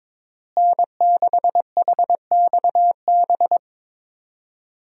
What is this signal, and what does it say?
Morse code, heard as a keyed tone, about 22 wpm, 700 Hz. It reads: N6HXB
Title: USA Call Signs - 5 Characters - Repeated 3x - 22wpm